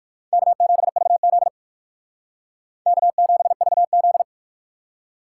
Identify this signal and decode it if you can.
Morse code, heard as a keyed tone, about 58 wpm, 700 Hz. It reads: K7VZ K7VZ